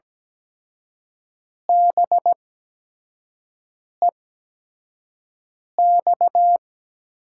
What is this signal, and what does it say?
Morse code, heard as a keyed tone, about 17 wpm, spaced as for 5 wpm, 700 Hz. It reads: BEX
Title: USA Call Signs - 6 Characters - 8x Character Spacing - 17wpm